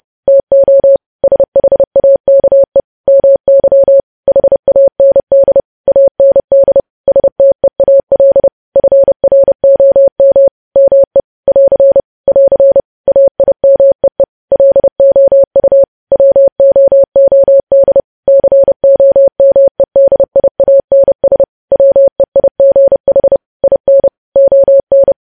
Morse code, heard as a keyed tone, about 30 wpm, 550 Hz. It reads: TO SHAKE MY HAND AND STEAL FROM ME <AR> <AR> AIMEE LOU WOOD COMEDIANS WEIGH IN ON
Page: news headlines in morse code at 30 wpm